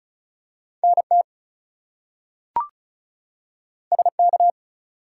Morse code, heard as a keyed tone, about 36 wpm, 700 Hz. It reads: NT SK
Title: US States and Canadian Provinces 35wpm